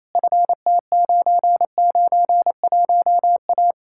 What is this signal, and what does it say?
Morse code, heard as a keyed tone, about 38 wpm, spaced as for 33 wpm, 700 Hz. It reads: FT991A
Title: Speed-Racing Common QSO Phrases Encore - 28wpm